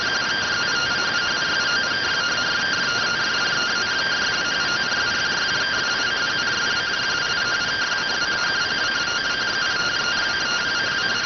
Two-tone teleprinter signal, about 170 Hz shift.